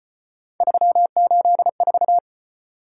Morse code, heard as a keyed tone, about 34 wpm, 700 Hz. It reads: 384